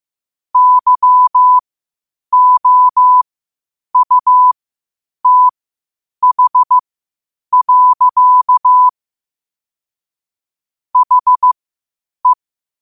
Morse code, heard as a keyed tone, about 15 wpm, spaced as for 8 wpm, 1000 Hz. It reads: YOUTH. HE
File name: qotdfiles / 8wpm